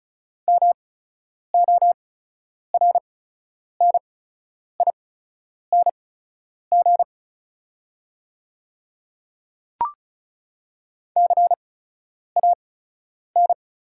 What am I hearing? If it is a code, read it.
Morse code, heard as a keyed tone, about 36 wpm, spaced as for 10 wpm, 690 Hz. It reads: MORNING CAN
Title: 8x Character Spacing - Top 500 Words - 35wpm